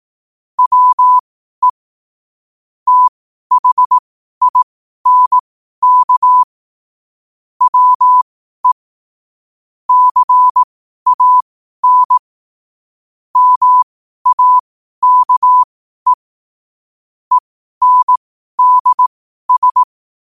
Morse code, heard as a keyed tone, about 18 wpm, spaced as for 12 wpm, 1000 Hz. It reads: WE THINK WE CAN MAKE ENDS
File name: qotdfiles / 12wpm